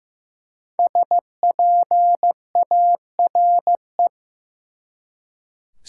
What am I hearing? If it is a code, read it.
Morse code, heard as a keyed tone, about 15 wpm, 700 Hz. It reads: SPARE
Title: Five Letter Words - 15wpm